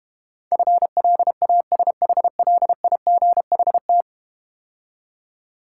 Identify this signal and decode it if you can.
Morse code, heard as a keyed tone, about 32 wpm, 700 Hz. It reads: FLASHLIGHT